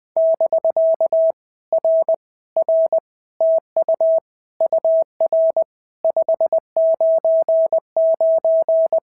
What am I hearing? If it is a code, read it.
Morse code, heard as a keyed tone, about 19 wpm, 650 Hz. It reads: <BK> R R TU UR 599
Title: POTA QSO Practice - 20wpm